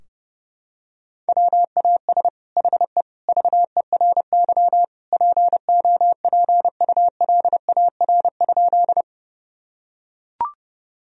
Morse code, heard as a keyed tone, about 30 wpm, 700 Hz. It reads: WAS HE VERY POPULAR?
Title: Sentences from Top 1000 Words - Review Part 3 - 30wpm